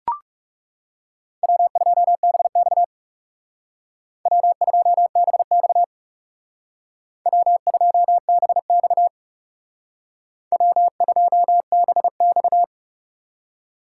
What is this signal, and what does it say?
Morse code, heard as a keyed tone, about 35 wpm, 700 Hz. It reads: W2BX W2BX W2BX W2BX
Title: Speed Racing - USA 1x2 Call Signs 30wpm